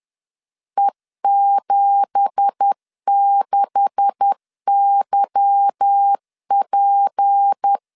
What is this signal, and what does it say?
Morse code, heard as a keyed tone, about 10 wpm, 800 Hz. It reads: E76YP
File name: LEZIONI / 9 2867 10wpm 2733